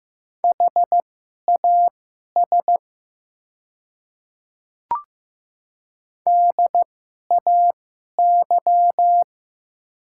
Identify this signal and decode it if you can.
Morse code, heard as a keyed tone, about 15 wpm, spaced as for 11 wpm, 690 Hz. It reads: HAS DAY
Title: 2x Character Spacing - Top 100 Words - 15wpm